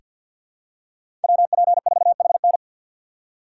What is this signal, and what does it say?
Morse code, heard as a keyed tone, about 46 wpm, 700 Hz. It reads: WP4HN
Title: USA Call Signs - 5 Characters - 3x Character Spacing - 50wpm